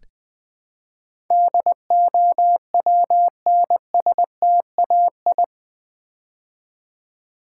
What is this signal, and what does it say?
Morse code, heard as a keyed tone, about 20 wpm, 700 Hz. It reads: DOWNSTAI
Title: Build A Word - 4 Letters At A Time - 10-Letter Words - 20wpm